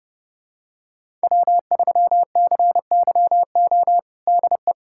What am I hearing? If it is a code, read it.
Morse code, heard as a keyed tone, about 30 wpm, 700 Hz. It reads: W3CYO DE
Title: USA Callsign DE Callsign - 5 Characters - 30wpm